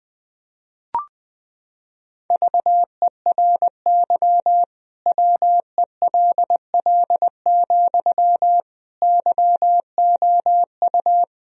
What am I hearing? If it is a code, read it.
Morse code, heard as a keyed tone, about 20 wpm, 700 Hz. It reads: VERY WELL, YOU
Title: Sentences from Top 1000 Words - Review Part 4 - 20wpm